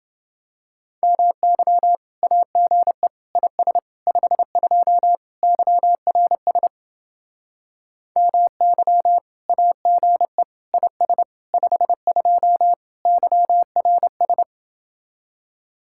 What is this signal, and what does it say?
Morse code, heard as a keyed tone, about 28 wpm, 700 Hz. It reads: MY AGE IS 52 YRS MY AGE IS 52 YRS